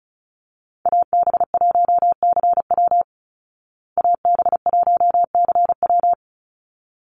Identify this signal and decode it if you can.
Morse code, heard as a keyed tone, about 34 wpm, 700 Hz. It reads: AB1CW AB1CW
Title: Call Signs Repeated 3x - 35wpm